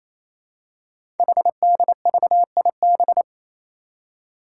Morse code, heard as a keyed tone, about 27 wpm, 700 Hz. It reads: HDVIB
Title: Code Groups - 5 Letters - 28wpm